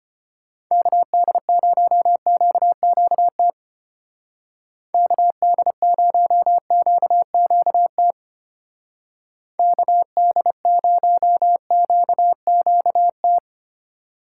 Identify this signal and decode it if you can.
Morse code, heard as a keyed tone, about 32 wpm, 700 Hz. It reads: KD0QQT KD0QQT KD0QQT